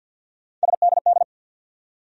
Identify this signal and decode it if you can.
Morse code, heard as a keyed tone, about 52 wpm, 700 Hz. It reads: SDD
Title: USA Call Sign Postfixes - 3 Letters 50wpm